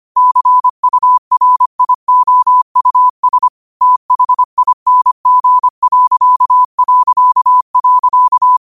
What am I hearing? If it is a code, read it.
Morse code, heard as a keyed tone, about 25 wpm, 1000 Hz. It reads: CURIOUS THING...
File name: qotdfiles / 25wpm